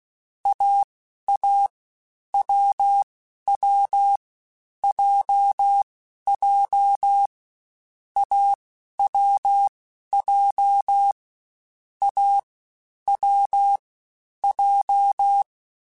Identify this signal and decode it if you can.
Morse code, heard as a keyed tone, about 16 wpm, spaced as for 11 wpm, 800 Hz. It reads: AA WW JJ AWJ A W J
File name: Difficult Combos / Difficult Combos-mp3s / AWJ1 @15